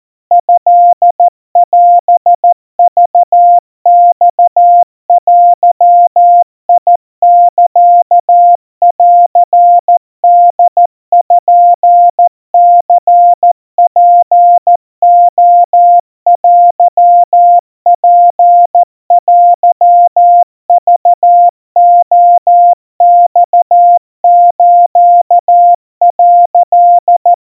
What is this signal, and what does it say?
Morse code (Japanese Wabun code), heard as a keyed tone, about 14 wpm, 700 Hz. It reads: トオクマデサンポニツレテツテクレマス」
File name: kawa / Morse sample / kamyu 50 cpm 700 Hz Crispy